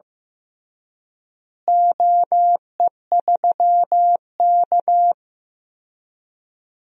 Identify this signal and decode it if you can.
Morse code, heard as a keyed tone, about 15 wpm, 700 Hz. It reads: OE3K
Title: DXCC-100 Call Signs - 4 Characters - Repeated 3x - 15wpm